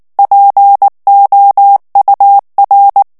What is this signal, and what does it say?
Morse code, heard as a keyed tone, about 19 wpm, 800 Hz. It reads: POUR
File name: morse mp3 / FRANCAIS TEXTE CLAIR 1 VITESSE 18 MOTS MINUTES